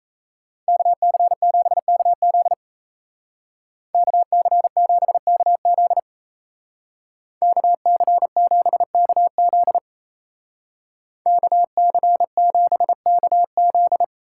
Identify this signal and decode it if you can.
Morse code, heard as a keyed tone, about 34 wpm, 700 Hz. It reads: KC7KZ KC7KZ KC7KZ KC7KZ
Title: Speed Racing - USA Call Signs 28wpm